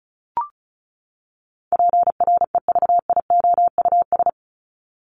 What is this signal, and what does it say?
Morse code, heard as a keyed tone, about 34 wpm, 700 Hz. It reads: PREVIOUS
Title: Top 1251-1500 Words 35wpm